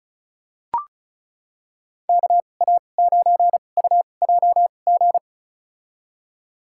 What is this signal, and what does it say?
Morse code, heard as a keyed tone, about 34 wpm, spaced as for 25 wpm, 700 Hz. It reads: KA9UJG
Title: USA Call Signs - 6 Characters - 2x Character Spacing - 35wpm